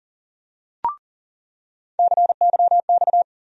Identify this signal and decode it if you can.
Morse code, heard as a keyed tone, about 38 wpm, 700 Hz. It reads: CYX
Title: USA Call Sign Postfixes - 3 Letters 40wpm